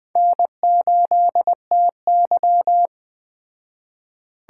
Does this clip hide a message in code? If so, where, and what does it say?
Yes: Morse code from 0.2 to 2.9 s, heard as a keyed tone, about 21 wpm, 700 Hz. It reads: N8TY